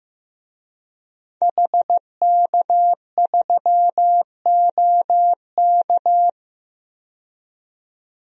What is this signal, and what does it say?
Morse code, heard as a keyed tone, about 15 wpm, 700 Hz. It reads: HK3OK
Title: DXCC-100 Call Signs - 5 Characters - 15wpm